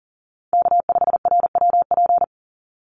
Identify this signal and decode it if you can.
Morse code, heard as a keyed tone, about 40 wpm, 700 Hz. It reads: K5RWP
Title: NA SOTA Chaser Callsigns 40wpm